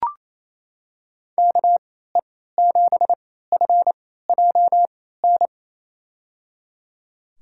Morse code, heard as a keyed tone, about 28 wpm, spaced as for 16 wpm, 700 Hz. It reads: KE7FJN
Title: USA Call Signs - 6 Characters - 3x Character Spacing - 28wpm